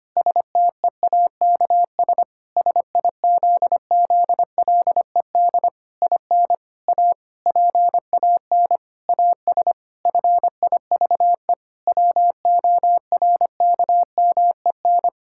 Morse code, heard as a keyed tone, about 25 wpm, 700 Hz. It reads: STEAKS SIZZLED IN A PAN AS FIVE WORKMEN
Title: Pangram Challenge v2 - 25wpm